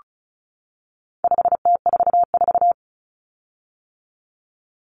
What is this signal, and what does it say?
Morse code, heard as a keyed tone, about 34 wpm, 700 Hz. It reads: 5T44
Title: Common Cut Numbers - 4-Digit 35wpm